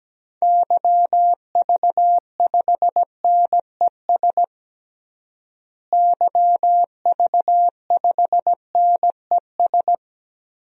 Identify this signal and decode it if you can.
Morse code, heard as a keyed tone, about 17 wpm, 700 Hz. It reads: YV5NES YV5NES
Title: DXCC-100 Call Signs - 6 Characters - Repeated 2x - 17wpm